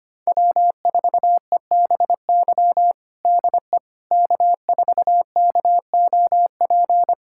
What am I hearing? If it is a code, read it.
Morse code, heard as a keyed tone, about 25 wpm, 700 Hz. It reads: W4EBY DE K4KOP
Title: USA Callsign DE Callsign - 5 Characters - 25wpm